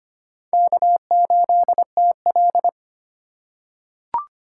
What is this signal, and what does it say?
Morse code, heard as a keyed tone, about 25 wpm, 700 Hz. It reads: K8TL
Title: USA Call Signs - 4 Characters - 3x Character Spacing - 25wpm